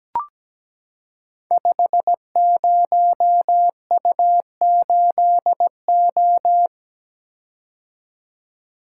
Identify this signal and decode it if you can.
Morse code, heard as a keyed tone, about 17 wpm, 700 Hz. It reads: 50U8O
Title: Code Groups - 5 Letter:Numbers - 17wpm